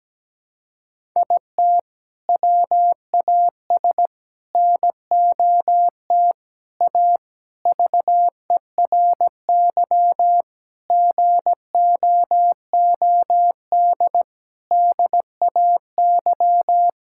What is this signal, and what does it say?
Morse code, heard as a keyed tone, about 17 wpm, 700 Hz. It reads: IT WAS NOT A VERY GOOD DAY
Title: Sentences from Top 1000 Words - Review Part 6 - 17wpm